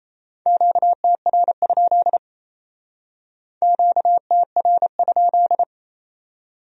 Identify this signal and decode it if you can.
Morse code, heard as a keyed tone, about 34 wpm, 700 Hz. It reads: QTR? QTR?
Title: Speed-Racing Q-Codes - 28wpm